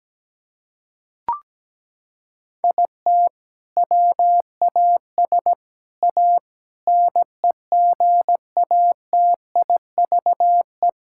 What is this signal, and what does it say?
Morse code, heard as a keyed tone, about 17 wpm, 700 Hz. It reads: IT WAS A NEGATIVE
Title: Sentences from Top 1000 Words - Review Part 6 - 17wpm